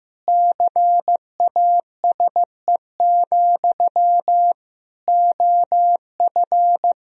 Morse code, heard as a keyed tone, about 15 wpm, 700 Hz. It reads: CASE, OF
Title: Sentences from Top 1000 Words - Review Part 3 - 15wpm